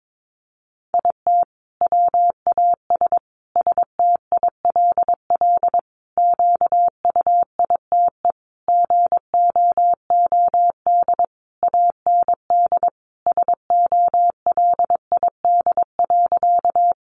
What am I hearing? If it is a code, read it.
Morse code, heard as a keyed tone, about 22 wpm, 700 Hz. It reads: IT WAS STILL QUITE GOOD AND SOLID.
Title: Sentences from Top 1500 Words 22wpm